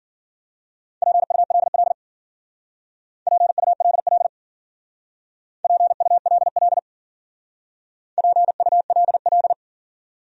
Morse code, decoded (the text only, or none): PULL PULL PULL PULL